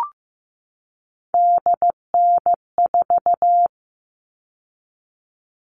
Morse code, heard as a keyed tone, about 15 wpm, 700 Hz. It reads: DN4
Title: All Cut Numbers - 3-Digit 15wpm